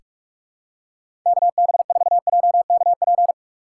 Morse code, heard as a keyed tone, about 46 wpm, 700 Hz. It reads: KB4JKP
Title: USA Call Signs - 6 Characters - 3x Character Spacing - 45wpm